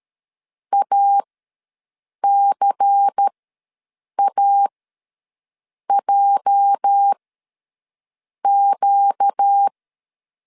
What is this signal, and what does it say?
Morse code, heard as a keyed tone, about 13 wpm, spaced as for 6 wpm, 800 Hz. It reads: ACAJQ
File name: LEZIONI / 8 YJ45 / 8 YJ ws 12wpm 3342